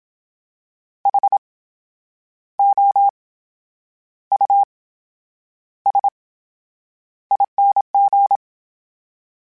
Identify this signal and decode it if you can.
Morse code, heard as a keyed tone, about 27 wpm, spaced as for 9 wpm, 800 Hz. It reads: HOUSING
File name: Special INGs / Special INGs-mp3s / VerySlowToFaster Ing @25